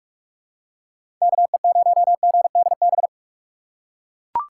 Morse code, heard as a keyed tone, about 43 wpm, 700 Hz. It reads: KE0GDB
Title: USA Call Signs - 6 Characters - 2x Character Spacing - 45wpm